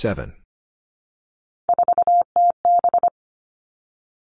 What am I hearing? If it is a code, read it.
Morse code, heard as a keyed tone, about 25 wpm, 700 Hz. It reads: 4TB
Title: All Cut Numbers - 3-Digit 25wpm